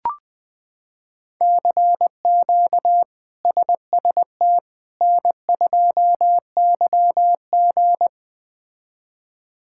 Morse code, heard as a keyed tone, about 20 wpm, 700 Hz. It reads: CQ SST N2YG
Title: K1USN SST QSO Practice - 20wpm